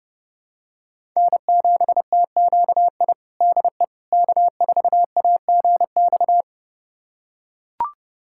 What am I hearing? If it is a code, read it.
Morse code, heard as a keyed tone, about 32 wpm, 700 Hz. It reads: N7TQI DE K4AGX
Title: USA Callsign DE Callsign 30wpm